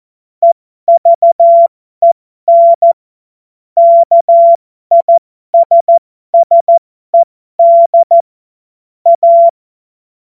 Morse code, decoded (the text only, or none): EVEN KISSED A